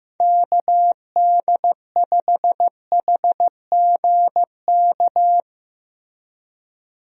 Morse code, heard as a keyed tone, about 15 wpm, 700 Hz. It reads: KD5HGK
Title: USA Call Signs - 6 Characters - Repeated 3x - 15wpm